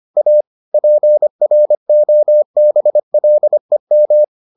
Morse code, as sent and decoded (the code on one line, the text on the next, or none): .- / .--. .-. --- -... .-.. --
A PROBLEM